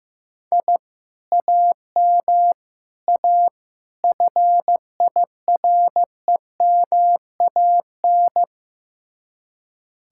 Morse code, heard as a keyed tone, about 16 wpm, 700 Hz. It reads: I AM A FIREMAN
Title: Speed-Racing Common QSO Phrases Encore - 15wpm